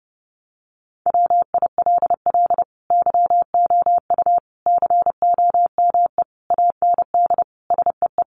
Morse code, heard as a keyed tone, about 30 wpm, 700 Hz. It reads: WILL YOU COME AND SEE
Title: Sentences from Top 100 Words 30wpm